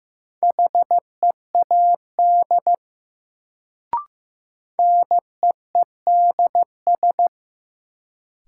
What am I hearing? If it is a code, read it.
Morse code, heard as a keyed tone, about 16 wpm, 700 Hz. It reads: HEAD NEEDS
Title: Sets of 3 Words - New Word Drill - 15wpm